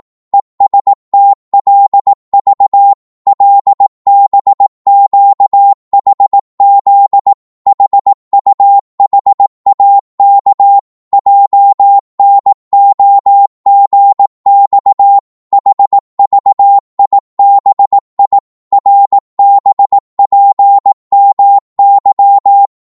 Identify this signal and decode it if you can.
Morse code, heard as a keyed tone, about 18 wpm, 800 Hz. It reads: ESTLV LBQHZ HUHAK JNOGX HVIBI RBPMY
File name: Morse / ltr-18wpm